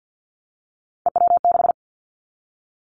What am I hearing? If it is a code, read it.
Morse code, heard as a keyed tone, about 50 wpm, 700 Hz. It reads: EW6
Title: Callsign Prefixes for Top 50 DXCC - 50wpm